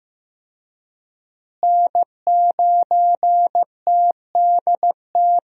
Morse code, heard as a keyed tone, about 15 wpm, 700 Hz. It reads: N9TDT
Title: USA Call Signs - 5 Characters - Repeated 3x - 15wpm